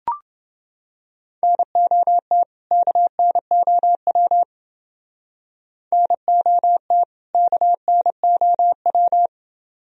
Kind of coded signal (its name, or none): Morse code